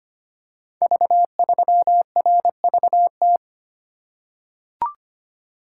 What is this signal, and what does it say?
Morse code, heard as a keyed tone, about 25 wpm, 700 Hz. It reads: V3RVT
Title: Code Groups - 5 Letter:Numbers - 25wpm